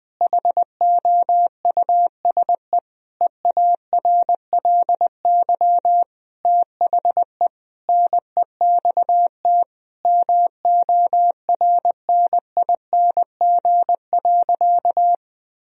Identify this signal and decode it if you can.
Morse code, heard as a keyed tone, about 21 wpm, 700 Hz. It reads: HOUSE EARLY THE NEXT MORNING.
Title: Sentences from Top 1000 Words - Review Part 2 - 20wpm